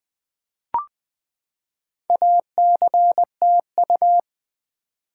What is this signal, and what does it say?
Morse code, heard as a keyed tone, about 20 wpm, 700 Hz. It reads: ACTU
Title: Build A Word - 4 Letters At A Time - 8-Letter Words - 20wpm